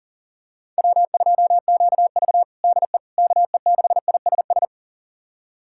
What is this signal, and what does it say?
Morse code, heard as a keyed tone, about 40 wpm, 700 Hz. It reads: W2QV DE KE6ISS
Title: USA Callsign DE Callsign 40wpm